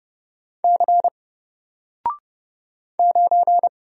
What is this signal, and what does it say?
Morse code, heard as a keyed tone, about 31 wpm, 700 Hz. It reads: C 9
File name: Single-Letter-Number-30wpm